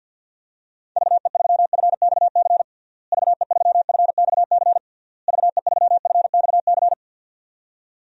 Morse code, heard as a keyed tone, about 49 wpm, 700 Hz. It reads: VE3FXC VE3FXC VE3FXC
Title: DXCC-100 Call Signs - 6 Characters - Repeated 3x - 50wpm